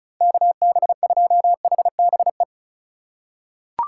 Morse code, heard as a keyed tone, about 36 wpm, 700 Hz. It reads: KB2HBE
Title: USA Call Signs - 4x Character Spacing - 35wpm